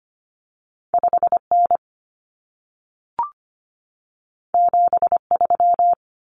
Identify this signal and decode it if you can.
Morse code, heard as a keyed tone, about 25 wpm, 700 Hz. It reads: 5N 73
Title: Common Cut Numbers - 2-Digit 25wpm